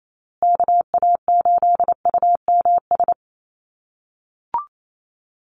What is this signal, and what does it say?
Morse code, heard as a keyed tone, about 28 wpm, 700 Hz. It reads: KA8UMS
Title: USA Call Signs Repeated 2x 28wpm